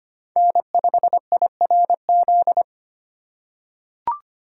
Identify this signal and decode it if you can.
Morse code, heard as a keyed tone, about 25 wpm, 700 Hz. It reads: N5IRZ
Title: USA Call Signs - 3x Character Spacing - 25wpm